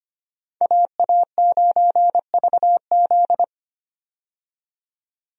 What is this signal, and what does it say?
Morse code, heard as a keyed tone, about 25 wpm, 700 Hz. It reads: AA9VZ
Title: USA Call Signs - 5 Characters - 25wpm